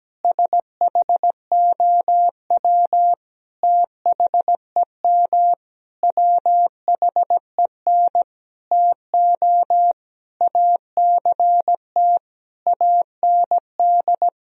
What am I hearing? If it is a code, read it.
Morse code, heard as a keyed tone, about 17 wpm, 700 Hz. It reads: SHOW THEM WHEN TO ACT AND
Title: Sentences from Top 1000 Words - Review Part 1 - 17wpm